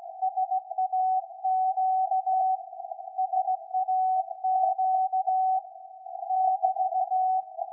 Morse code, heard as a keyed tone, about 16 wpm, 750 Hz. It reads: SAQ SAQ <BT>
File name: iz7slz / VLF / SAQ 13-02-2023